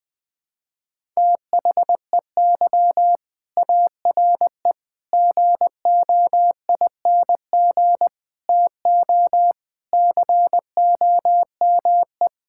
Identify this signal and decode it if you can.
Morse code, heard as a keyed tone, about 20 wpm, 700 Hz. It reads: THEY ARE GOING TO COME
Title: Sentences with 3-Word Phrases - 20wpm